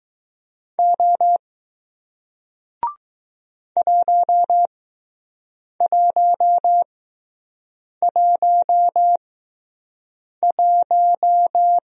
Morse code, heard as a keyed tone, about 20 wpm, spaced as for 6 wpm, 700 Hz. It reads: O 1111